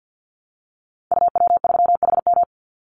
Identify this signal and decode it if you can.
Morse code, heard as a keyed tone, about 50 wpm, 700 Hz. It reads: UW3HM
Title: Call Signs - 50wpm